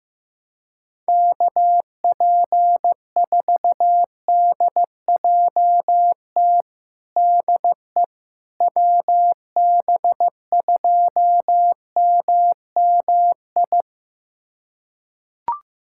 Morse code, heard as a keyed tone, about 15 wpm, 700 Hz. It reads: KP4DJT DE WB2MMI
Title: USA Callsign DE Callsign - 6 Characters - 15wpm